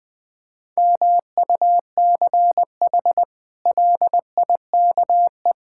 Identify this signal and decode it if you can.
Morse code, heard as a keyed tone, about 21 wpm, 700 Hz. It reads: MUCH LIKE
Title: Sentences with 8-Word Phrases - 20wpm